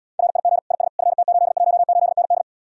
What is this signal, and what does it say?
Morse code, heard as a keyed tone, about 80 wpm, 700 Hz. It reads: RIG IS FT818ND